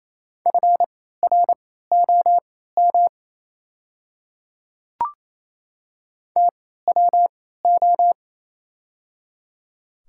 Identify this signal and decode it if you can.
Morse code, heard as a keyed tone, about 28 wpm, spaced as for 16 wpm, 700 Hz. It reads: FROM TWO
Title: 3x Character Spacing - Top 100 Words - 28wpm